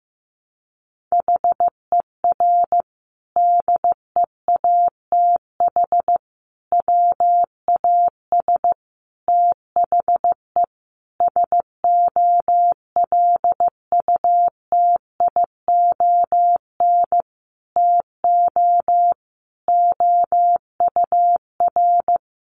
Morse code, heard as a keyed tone, about 14 wpm, 700 Hz. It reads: HER DEATH WAS THE SOLUTION TO OUR
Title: Sentences from Top 1300 Words 15wpm